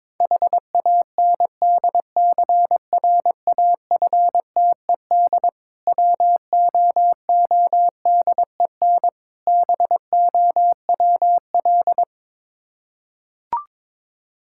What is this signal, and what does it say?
Morse code, heard as a keyed tone, about 22 wpm, 690 Hz. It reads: HANDCRAFTED WOODEN BOWL
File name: Difficult Words with Hints - 22wpm